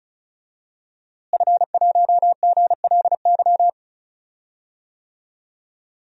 Morse code, heard as a keyed tone, about 36 wpm, 700 Hz. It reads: F1GLY
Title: DXCC-100 Call Signs - 5 Characters - Repeated 2x - 35wpm